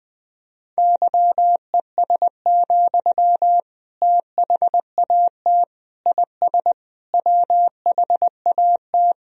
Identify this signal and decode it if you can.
Morse code, heard as a keyed tone, about 20 wpm, 700 Hz. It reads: YES, THAT IS WHAT
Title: Sentences from Top 500 Words and Abbreviations 20wpm